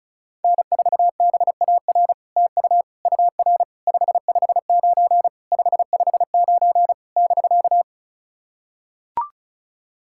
Morse code, heard as a keyed tone, about 36 wpm, 700 Hz. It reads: N4BAR TU UR 559 559 <BK>
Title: SOTA QSO Practice 35wpm